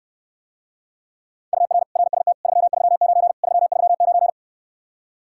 Morse code, heard as a keyed tone, about 69 wpm, 700 Hz. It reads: UR RST 339 339